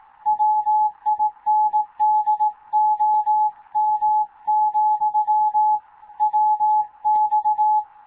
Morse code, heard as a keyed tone, about 18 wpm, 850 Hz. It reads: WINDOM, WX
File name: Morze-HA2MN / hangp / 2 11 GYAKORLAT